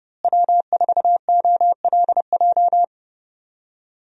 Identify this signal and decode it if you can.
Morse code, heard as a keyed tone, about 30 wpm, 700 Hz. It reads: W4OLJ
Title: USA Call Signs - 5 Characters - Repeated 2x - 30wpm